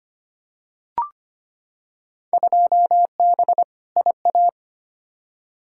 Morse code, heard as a keyed tone, about 25 wpm, 700 Hz. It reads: 2B IA
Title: Field Day Exchanges Encore - 25wpm